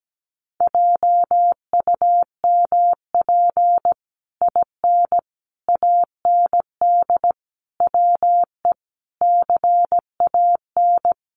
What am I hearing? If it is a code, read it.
Morse code, heard as a keyed tone, about 17 wpm, 700 Hz. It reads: JUMP IN AND WE CAN
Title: Sentences from Top 1300 Words 17wpm